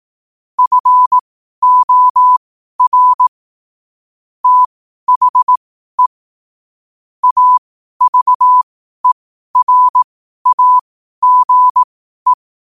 Morse code, heard as a keyed tone, about 18 wpm, spaced as for 12 wpm, 1000 Hz. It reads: FOR THE AVERAGE
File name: qotdfiles / 12wpm